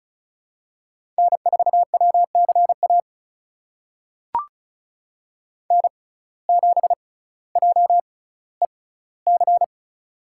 Morse code, heard as a keyed tone, about 36 wpm, spaced as for 18 wpm, 700 Hz. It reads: N4WCA N7JEC
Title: USA Call Signs - 5 Characters - 6x Character Spacing - 35wpm